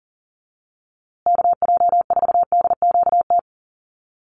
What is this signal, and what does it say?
Morse code, heard as a keyed tone, about 40 wpm, 700 Hz. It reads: KJ4DQT
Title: USA Call Signs Repeated 3x 40wpm